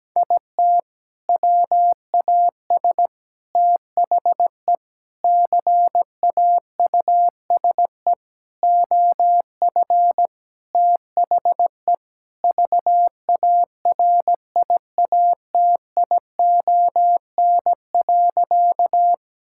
Morse code, heard as a keyed tone, about 17 wpm, 700 Hz. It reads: IT WAS THE CAUSE OF THE VARIATION.
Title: Sentences from Top 1600 Words 17wpm